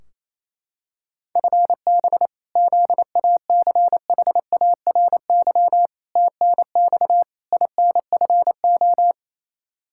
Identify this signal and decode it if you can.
Morse code, heard as a keyed tone, about 27 wpm, 700 Hz. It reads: FB ZACHARY TNX INFO
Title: Practice QSO - Exchange 3 - More Info - 28wpm